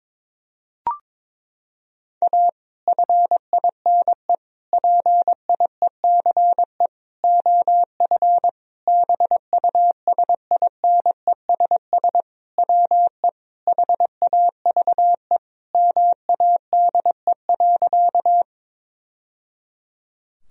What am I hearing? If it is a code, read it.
Morse code, heard as a keyed tone, about 22 wpm, 700 Hz. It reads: A FINE PIECE OF BUSINESS WE HAVE MADE.
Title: Sentences from Top 1000 Words - Review Part 2 - 22wpm